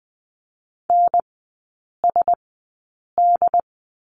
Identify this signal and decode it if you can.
Morse code, heard as a keyed tone, about 20 wpm, spaced as for 8 wpm, 700 Hz. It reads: NSD
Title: 074 Lesson 13e - Letter L - Three Characters 20wpm